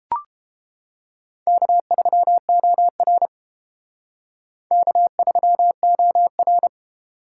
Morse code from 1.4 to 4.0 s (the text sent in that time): K3OR